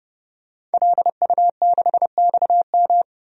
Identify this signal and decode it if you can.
Morse code, heard as a keyed tone, about 29 wpm, 690 Hz. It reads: LU6XM
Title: DXCC-100 Call Signs - 5 Characters - 30wpm